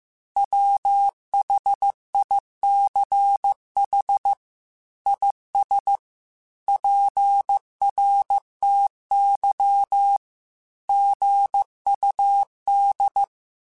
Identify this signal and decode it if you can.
Morse code, heard as a keyed tone, about 15 wpm, 800 Hz. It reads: WHICH IS PRTY GUD
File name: QSOs / QSOs-mp3s / WA7QOX 1@13